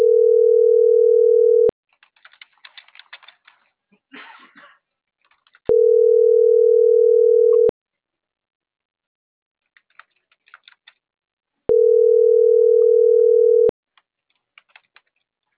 Ringback tone (North America), 440 + 480 Hz.
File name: CENTRAL MABE / out vanesa.mabe